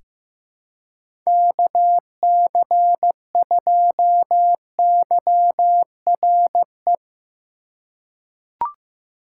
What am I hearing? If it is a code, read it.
Morse code, heard as a keyed tone, about 15 wpm, 700 Hz. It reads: KC2YRE
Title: USA Call Signs - 6 Characters - 2x Character Spacing - 15wpm